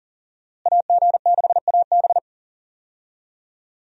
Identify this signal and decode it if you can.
Morse code, heard as a keyed tone, about 40 wpm, 700 Hz. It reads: AG6AB
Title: USA Call Signs - 5 Characters - Repeated 3x - 40wpm